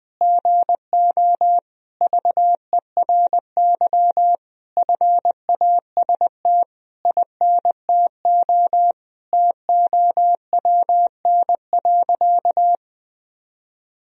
Morse code, heard as a keyed tone, about 20 wpm, 700 Hz. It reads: GO VERY FAST INTO TOWN.
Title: Sentences from Top 1000 Words - Review Part 2 - 20wpm